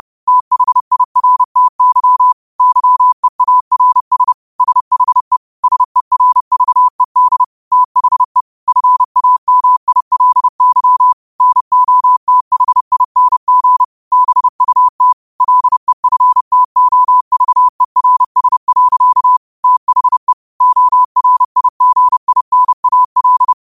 Morse code, heard as a keyed tone, about 30 wpm, 1000 Hz. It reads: THIRTY YEARS SHE SERVED THE FAMILY NOTHING BUT LEFTOVERS. THE ORIGINAL